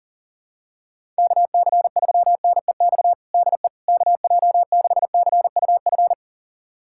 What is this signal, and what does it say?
Morse code, heard as a keyed tone, about 40 wpm, 700 Hz. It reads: KC3NEX DE KJ6CUF
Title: USA Callsign DE Callsign - 6 Characters - 40wpm